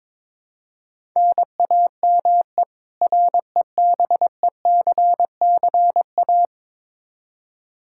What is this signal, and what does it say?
Morse code, heard as a keyed tone, about 21 wpm, 700 Hz. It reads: NAME REBECCA
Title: K1USN SST QSO Practice 22wpm